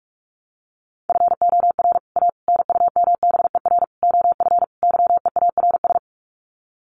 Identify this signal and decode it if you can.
Morse code, heard as a keyed tone, about 44 wpm, 700 Hz. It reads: FOR A NUMBER OF YEARS
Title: Sets of 5 Words Encore 45wpm